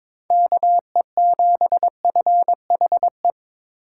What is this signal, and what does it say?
Morse code, heard as a keyed tone, about 22 wpm, 700 Hz. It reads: KE7FHE
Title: USA Call Signs - 6 Characters - 2x Character Spacing - 22wpm